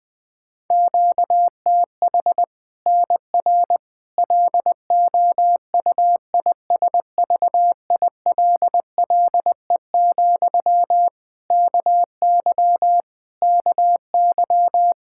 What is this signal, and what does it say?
Morse code, heard as a keyed tone, about 20 wpm, 700 Hz. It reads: QTH NR LOUISVILLE, KY KY
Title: K1USN SST QSO Practice 20wpm